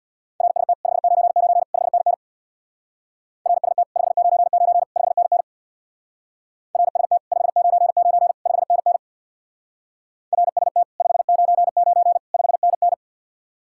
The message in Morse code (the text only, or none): RST 599 5NN RST 599 5NN RST 599 5NN RST 599 5NN